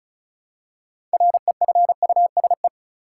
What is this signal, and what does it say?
Morse code, heard as a keyed tone, about 34 wpm, 700 Hz. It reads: REFUSE
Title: Six Letter Words - 35wpm